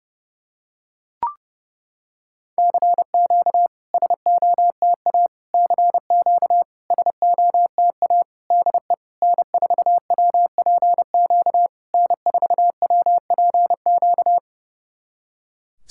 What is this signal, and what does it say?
Morse code, heard as a keyed tone, about 30 wpm, 690 Hz. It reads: CQ SOTA CQ SOTA DE N4WPQ N4WPQ
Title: SOTA QSO Practice 30wpm